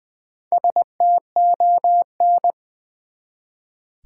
Morse code, heard as a keyed tone, about 20 wpm, 700 Hz. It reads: STON
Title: Build A Word - 1 Letter At A Time - 5-Letter Words - 20wpm